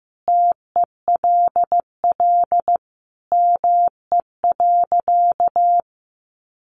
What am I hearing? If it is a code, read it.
Morse code, heard as a keyed tone, about 15 wpm, 700 Hz. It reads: TELL ME.
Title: Sentences from Top 500 Words 15wpm